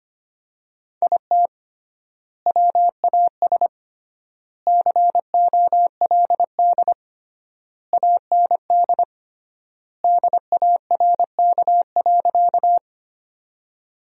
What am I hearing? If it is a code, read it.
Morse code, heard as a keyed tone, about 25 wpm, 700 Hz. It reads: IT WAS COLD AND DARK.
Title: 3x Word-Spacing - Sentences from Top 500 Words - 25wpm